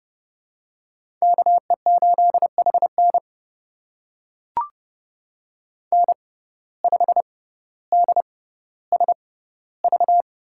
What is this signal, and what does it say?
Morse code, heard as a keyed tone, about 30 wpm, spaced as for 15 wpm, 700 Hz. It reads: KE8HN N5DSV
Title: USA Call Signs - 5 Characters - 6x Character Spacing - 30wpm